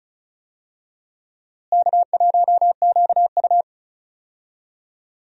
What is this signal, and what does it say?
Morse code, heard as a keyed tone, about 36 wpm, 700 Hz. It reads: K1QU
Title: USA Call Signs - 4 Characters - Repeated 3x - 35wpm